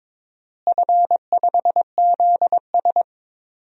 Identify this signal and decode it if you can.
Morse code, heard as a keyed tone, about 22 wpm, 700 Hz. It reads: F5ZS
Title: DXCC-100 Call Signs - 4 Characters - Repeated 2x - 22wpm